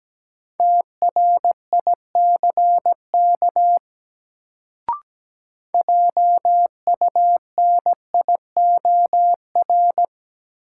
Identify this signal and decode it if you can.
Morse code, heard as a keyed tone, about 17 wpm, 700 Hz. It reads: TRICK JUNIOR
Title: Sentences from Top 1500 Words - New Word Drill Alt - 17wpm